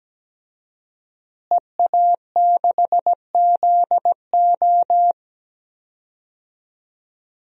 Morse code, heard as a keyed tone, about 17 wpm, 700 Hz. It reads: EA6ZO
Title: DXCC-100 Call Signs - 5 Characters - Repeated 3x - 17wpm